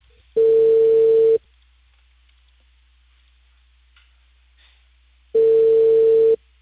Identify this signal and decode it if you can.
Ringback tone (China, 450 Hz).